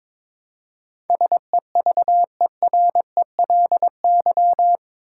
Morse code, heard as a keyed tone, about 22 wpm, 700 Hz. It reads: SEVERELY
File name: practice / 22wpm / Eight Letter Words - 22wpm